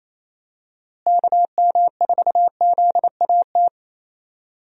Morse code, heard as a keyed tone, about 28 wpm, 700 Hz. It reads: KM4ZAT
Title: USA Call Signs - 6 Characters - Repeated 3x - 28wpm